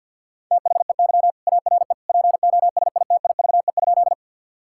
Morse code, heard as a keyed tone, about 50 wpm, 700 Hz. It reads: THEY ARE POSITIVE?
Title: Sentences from Top 1000 Words - Review Part 5 - 50wpm